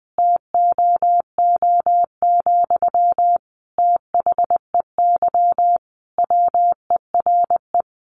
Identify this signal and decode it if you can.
Morse code, heard as a keyed tone, about 20 wpm, 700 Hz. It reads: TOO, THEY WERE
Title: Sentences from Top 1200 Words 20wpm